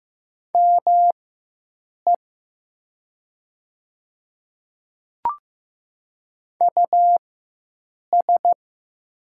Morse code, heard as a keyed tone, about 15 wpm, spaced as for 7 wpm, 700 Hz. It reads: ME US
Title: 4x Character Spacing - Top 200 Words - 15wpm